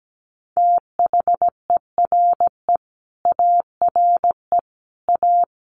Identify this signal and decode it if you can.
Morse code, heard as a keyed tone, about 17 wpm, 700 Hz. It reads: THERE ARE A
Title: Sets of 4 Words 17wpm